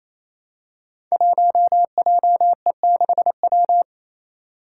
Morse code, heard as a keyed tone, about 28 wpm, 700 Hz. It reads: 1JE6W
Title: Five Challenging Letter-Number Combinations 28wpm